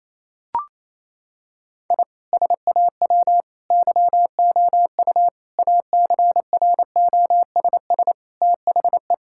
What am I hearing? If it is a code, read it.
Morse code, heard as a keyed tone, about 28 wpm, 700 Hz. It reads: I SAW YOU ACROSS THE
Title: Sentences from Top 1000 Words - Review Part 5 - 28wpm